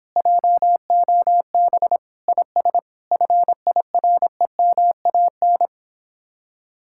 Morse code, heard as a keyed tone, about 26 wpm, 700 Hz. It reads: JOB IS FIREMAN